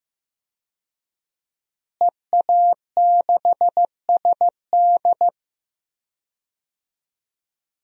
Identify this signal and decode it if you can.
Morse code, heard as a keyed tone, about 15 wpm, 700 Hz. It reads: EA6SD
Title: DXCC-100 Call Signs - 5 Characters - Repeated 3x - 15wpm